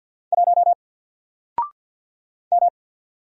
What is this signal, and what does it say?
Morse code, heard as a keyed tone, about 48 wpm, 700 Hz. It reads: 1 M
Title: Single Letter-Number Alt 50wpm